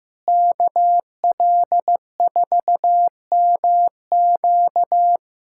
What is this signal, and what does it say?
Morse code, heard as a keyed tone, about 15 wpm, 700 Hz. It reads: KL4MQ
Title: USA Callsign DE Callsign - 5 Characters - 15wpm